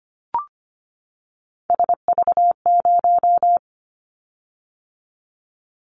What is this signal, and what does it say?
Morse code, heard as a keyed tone, about 25 wpm, 700 Hz. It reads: SV0
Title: Callsign Prefixes for Top 50 DXCC - 25wpm